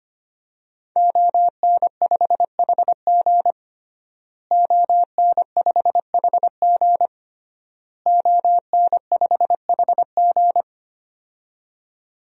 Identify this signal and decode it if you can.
Morse code, heard as a keyed tone, about 25 wpm, 700 Hz. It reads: ON5HG ON5HG ON5HG
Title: DXCC-100 Call Signs - 5 Characters - Repeated 3x - 25wpm